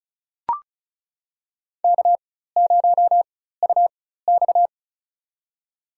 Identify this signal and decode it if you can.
Morse code, heard as a keyed tone, about 34 wpm, spaced as for 16 wpm, 700 Hz. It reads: K0UX
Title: USA Call Signs - 4 Characters - 4x Character Spacing - 35wpm